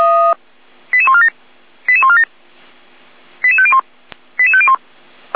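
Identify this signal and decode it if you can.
Five-tone selective call (ZVEI), heard as quick runs of five tones, tones of about 70 ms. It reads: 88147 88147 88501 88501
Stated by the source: Five-tone ZVEI selective call received by SDR.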